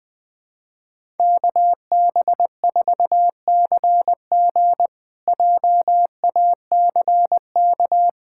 Morse code, heard as a keyed tone, about 20 wpm, 700 Hz. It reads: KB4CG JACK
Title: K1USN SST QSO Practice - 20wpm